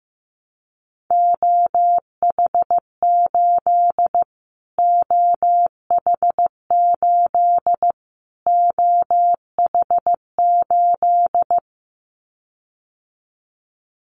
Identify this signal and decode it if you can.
Morse code, heard as a keyed tone, about 15 wpm, 700 Hz. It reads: OH8 OH8 OH8
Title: Callsign Prefixes for Top 25 DXCC - Repeated 3x - 15wpm